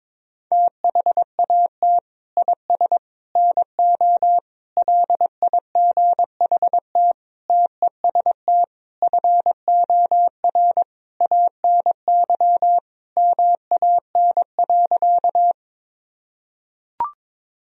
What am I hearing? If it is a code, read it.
Morse code, heard as a keyed tone, about 22 wpm, 690 Hz. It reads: THAT IS NO LIGHT TEST FOR ANY MAN.
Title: Sentences from Top 1000 Words - Review Part 3 - 22wpm